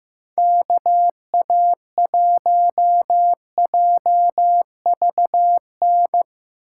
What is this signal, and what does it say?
Morse code, heard as a keyed tone, about 15 wpm, 700 Hz. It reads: KA1JVN
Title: USA Call Signs - 6 Characters - 2x Character Spacing - 15wpm